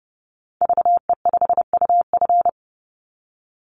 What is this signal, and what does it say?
Morse code, heard as a keyed tone, about 32 wpm, 700 Hz. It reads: VE5UF
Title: Call Signs - 30wpm